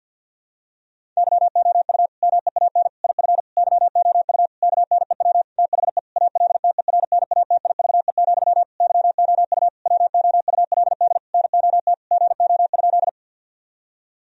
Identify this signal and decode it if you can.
Morse code, heard as a keyed tone, about 52 wpm, 700 Hz. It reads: YOU MEAN IF YOU KNEW THE ALTERNATIVE, YOU WOULD NOT GO?
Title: Sentences from Top 1000 Words - Review Part 4 - 50wpm